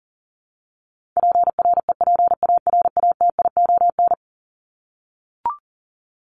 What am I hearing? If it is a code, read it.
Morse code, heard as a keyed tone, about 40 wpm, 700 Hz. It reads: PREPARATION
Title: Top 1251-1500 Words 40wpm